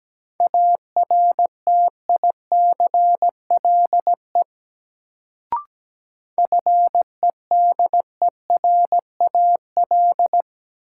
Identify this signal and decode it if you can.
Morse code, heard as a keyed tone, about 17 wpm, 700 Hz. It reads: ARTICLE FEDERAL
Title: Sentences from Top 1000 Words - Review Part 5 - New Word Drill - 17wpm